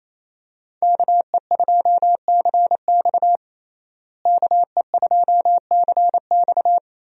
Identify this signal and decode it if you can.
Morse code, heard as a keyed tone, about 28 wpm, 700 Hz. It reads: KE2CX KE2CX
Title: USA Call Signs - 5 Characters - Repeated 2x - 28wpm